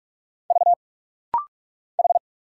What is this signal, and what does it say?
Morse code, heard as a keyed tone, about 46 wpm, 700 Hz. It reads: V H
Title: Single Letters - Rapid-Fire 45wpm